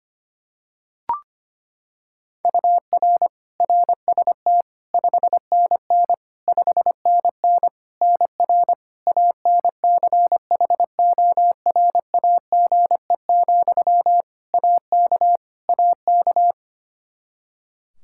Morse code, heard as a keyed tone, about 25 wpm, 700 Hz. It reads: UR RST 5NN 5NN NR ANCHORAGE, AK AK
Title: Practice QSO - Exchange 1 - Beginning - 25wpm